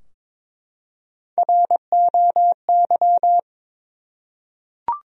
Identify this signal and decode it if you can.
Morse code, heard as a keyed tone, about 22 wpm, 700 Hz. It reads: ROY
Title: USA Common Names - Max 3 Letters - 22wpm